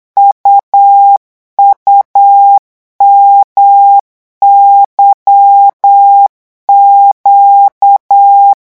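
Morse code, heard as a keyed tone, about 8 wpm, 800 Hz. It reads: UUMYQ
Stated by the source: Morse code revision lesson at 8 words per minute, from a French amateur radio course.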